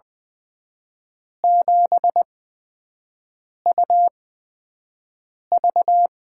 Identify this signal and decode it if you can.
Morse code, heard as a keyed tone, about 21 wpm, spaced as for 6 wpm, 700 Hz. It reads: 7UV
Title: USA Call Signs - 4 Characters - 8x Character Spacing - 20wpm